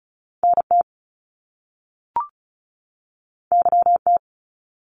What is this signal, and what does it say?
Morse code, heard as a keyed tone, about 36 wpm, 700 Hz. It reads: NT YT
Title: Canadian Provinces and Territories 35wpm